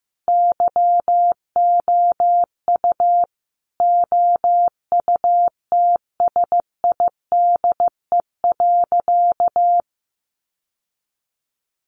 Morse code, heard as a keyed tone, about 15 wpm, 700 Hz. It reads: YOU OUTSIDE.
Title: Sentences from Top 900 Words 15wpm